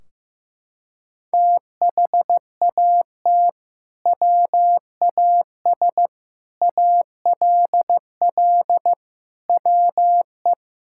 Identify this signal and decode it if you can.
Morse code, heard as a keyed tone, about 15 wpm, 700 Hz. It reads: THAT WAS ALL WE
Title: Sentences from Top 1600 Words 15wpm